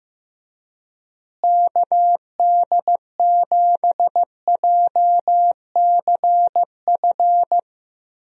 Morse code, heard as a keyed tone, about 15 wpm, 700 Hz. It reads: KD7JCF
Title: USA Call Signs - 6 Characters - Repeated 2x - 15wpm